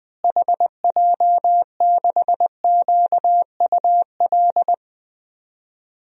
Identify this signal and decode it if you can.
Morse code, heard as a keyed tone, about 20 wpm, 700 Hz. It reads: HJ6QUL
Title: DXCC-100 Call Signs - 6 Characters - Repeated 3x - 20wpm